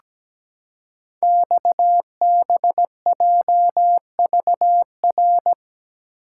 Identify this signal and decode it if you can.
Morse code, heard as a keyed tone, about 17 wpm, 700 Hz. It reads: XBJVR
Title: Code Groups - 5 Letters - 17wpm